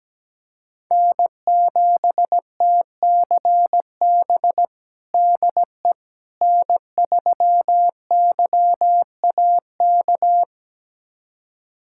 Morse code, heard as a keyed tone, about 17 wpm, 700 Hz. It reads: N7TCB DE N3YAK
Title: USA Callsign DE Callsign - 5 Characters - 17wpm